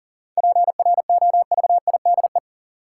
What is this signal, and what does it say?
Morse code, heard as a keyed tone, about 40 wpm, 700 Hz. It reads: PROVIDE
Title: Sentences from Top 1000 Words - Review Part 2 - New Word Drill - 40wpm